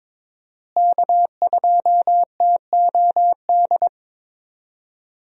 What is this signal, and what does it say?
Morse code, heard as a keyed tone, about 22 wpm, 700 Hz. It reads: K2TOD
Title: USA Call Signs - 5 Characters - Repeated 2x - 22wpm